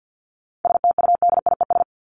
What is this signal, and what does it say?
Morse code, heard as a keyed tone, about 50 wpm, 700 Hz. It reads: STUDIES
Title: ES:S Suffix - Top 100 Words with Base Words 50wpm